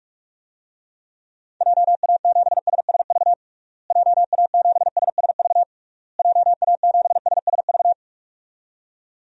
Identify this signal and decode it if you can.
Morse code, heard as a keyed tone, about 41 wpm, 700 Hz. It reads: JA7SSV JA7SSV JA7SSV
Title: DXCC-100 Call Signs - 6 Characters - Repeated 3x - 45wpm